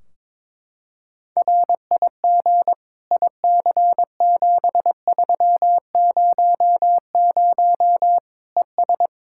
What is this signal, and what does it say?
Morse code, heard as a keyed tone, about 22 wpm, 700 Hz. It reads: RIG IC7300 ES
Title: Practice QSO - Exchange 2 - Info - 22wpm